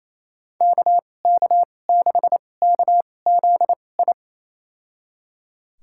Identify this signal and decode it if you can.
Morse code, heard as a keyed tone, about 28 wpm, spaced as for 20 wpm, 700 Hz. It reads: KK6KZI
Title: USA Call Signs - 6 Characters - 2x Character Spacing - 28wpm